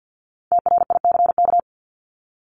Morse code, heard as a keyed tone, about 50 wpm, 700 Hz. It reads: TRICK
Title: Top 1251-1500 Words 50wpm